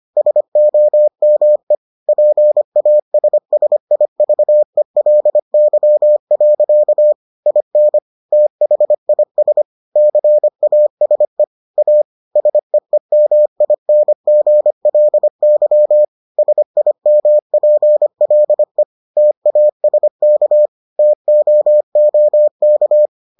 Morse code, heard as a keyed tone, about 25 wpm, 600 Hz. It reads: SOME PASSIVELY. IN THIS CASE A SEEMINGLY SIMPLE TASK TOOK